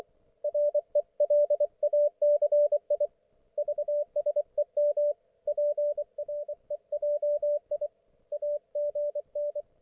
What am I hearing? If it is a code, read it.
Morse code, heard as a keyed tone, about 24 wpm, 590 Hz, with radio noise. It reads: RELACI VSEM PREJI AGN
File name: Archiv / CUC212 / ok1fhi cul212